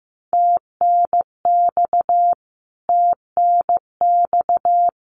Morse code, heard as a keyed tone, about 15 wpm, 700 Hz. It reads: TNX TNX
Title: Abbreviations Repeated 3x - 15wpm